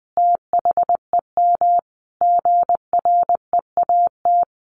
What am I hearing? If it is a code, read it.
Morse code, heard as a keyed tone, about 20 wpm, 700 Hz. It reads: THEM GREAT
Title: Sentences from Top 1300 Words 20wpm